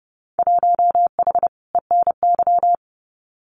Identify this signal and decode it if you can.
Morse code, heard as a keyed tone, about 30 wpm, 700 Hz. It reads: 1H ENY
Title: Winter Field Day Exchanges 30wpm